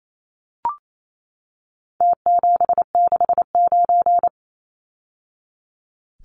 Morse code, heard as a keyed tone, about 28 wpm, 700 Hz. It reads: T769
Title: Numbers - Contest Style - 4-Digit 28wpm